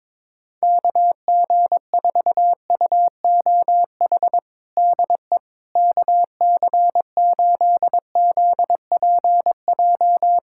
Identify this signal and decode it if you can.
Morse code, heard as a keyed tone, about 22 wpm, 700 Hz. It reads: KG4UOH DE KC8ZPJ
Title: USA Callsign DE Callsign - 6 Characters - 22wpm